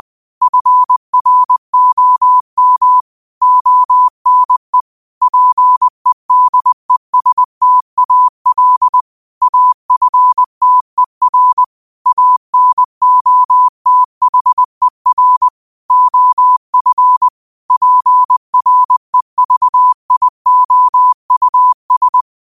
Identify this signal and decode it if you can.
Morse code, heard as a keyed tone, about 20 wpm, 1000 Hz. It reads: FROM ONE PEDESTAL AFTER ANOTHER OF PREVIOUS